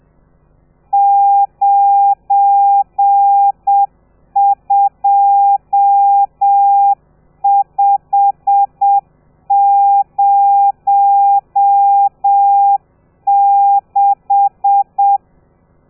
Morse code, heard as a keyed tone, about 7 wpm, 800 Hz. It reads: 92506
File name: telegrafia / i2tfj / Lez12 30